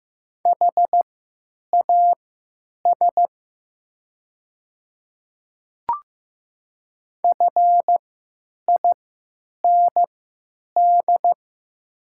Morse code, heard as a keyed tone, about 15 wpm, spaced as for 9 wpm, 690 Hz. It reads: HAS FIND
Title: 3x Character Spacing - Top 100 Words - 15wpm